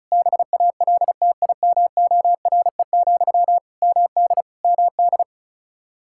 Morse code, heard as a keyed tone, about 36 wpm, 700 Hz. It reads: BALTIMORE, MD MD
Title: Practice QSO - Exchange 1 - Beginning - 35wpm